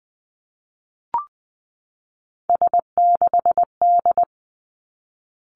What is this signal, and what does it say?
Morse code, heard as a keyed tone, about 20 wpm, 700 Hz. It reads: S6D
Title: Three Challenging Letter-Number Combinations 20wpm